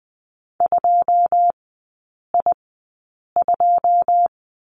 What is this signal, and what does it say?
Morse code, heard as a keyed tone, about 20 wpm, spaced as for 8 wpm, 700 Hz. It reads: 2I2
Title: 081 Lesson 14e - Number 2 - Three Characters 20wpm